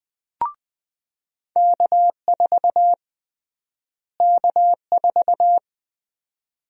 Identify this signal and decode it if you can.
Morse code, heard as a keyed tone, about 20 wpm, 700 Hz. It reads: K4 K4
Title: USA Call Sign Prefixes - Repeated 2x - 20wpm